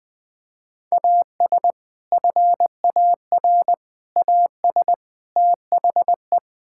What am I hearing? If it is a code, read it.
Morse code, heard as a keyed tone, about 20 wpm, 700 Hz. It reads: AS FAR AS THE
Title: Build A Phrase - 4-Word Phrases - 20wpm